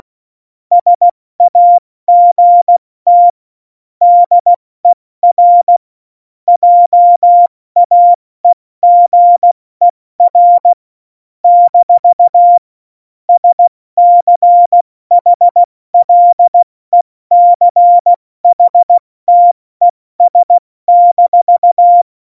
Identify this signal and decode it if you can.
Morse code, heard as a keyed tone, about 16 wpm, 700 Hz. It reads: SAGT DER JAEGER - SCHLECHTES-
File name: Morsekurs / Witze / Witz 27 Juni 17, Tempo 70 F80